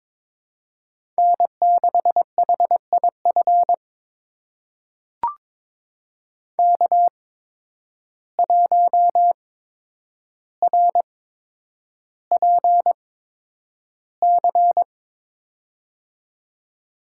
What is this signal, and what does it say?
Morse code, heard as a keyed tone, about 22 wpm, spaced as for 9 wpm, 700 Hz. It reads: N6HIF K1RPC
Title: USA Call Signs - 5 Characters - 8x Character Spacing - 22wpm